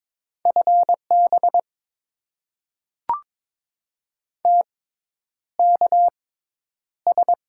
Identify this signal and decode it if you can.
Morse code, heard as a keyed tone, about 22 wpm, spaced as for 10 wpm, 700 Hz. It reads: FB TKS
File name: Top 90 QSO Elements - 6x Character Spacing - 22wpm